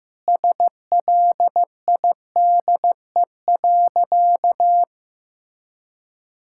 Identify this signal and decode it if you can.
Morse code, heard as a keyed tone, about 15 wpm, 700 Hz. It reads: SLIDE.
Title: Sentences from Top 1600 Words 15wpm